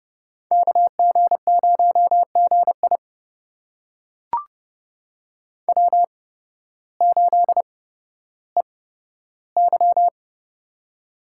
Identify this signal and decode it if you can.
Morse code, heard as a keyed tone, about 30 wpm, spaced as for 14 wpm, 700 Hz. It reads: KG0GI W8EY
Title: USA Call Signs - 8x Character Spacing - 30wpm